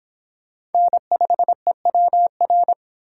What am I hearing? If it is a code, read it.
Morse code, heard as a keyed tone, about 25 wpm, 700 Hz. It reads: N5EWR